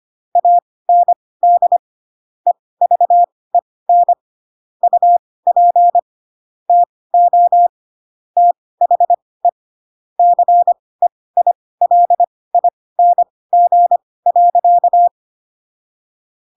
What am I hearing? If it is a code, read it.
Morse code, heard as a keyed tone, about 25 wpm, spaced as for 18 wpm, 700 Hz. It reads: AND EVEN UP TO THE CEILING.
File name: Pinocchio / Pinocchio ch1 12 25 ik1hge